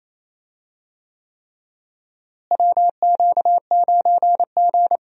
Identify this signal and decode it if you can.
Morse code, heard as a keyed tone, about 28 wpm, 700 Hz. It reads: WQ9G
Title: USA Call Signs - 4 Characters - Repeated 3x - 28wpm